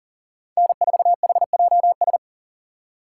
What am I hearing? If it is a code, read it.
Morse code, heard as a keyed tone, about 52 wpm, 700 Hz. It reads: N4HJS